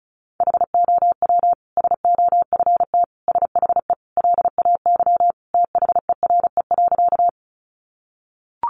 Morse code, heard as a keyed tone, about 34 wpm, 700 Hz. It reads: HOW SOFT SHE LAY THERE.
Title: Sentences from Top 1500 Words 35wpm